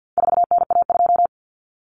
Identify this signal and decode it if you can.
Morse code, heard as a keyed tone, about 50 wpm, 700 Hz. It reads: 4NA2
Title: Common Cut Numbers - 4-Digit 50wpm